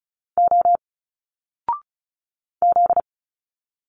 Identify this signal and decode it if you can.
Morse code, heard as a keyed tone, about 36 wpm, 700 Hz. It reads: O Z
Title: Single Letters 35wpm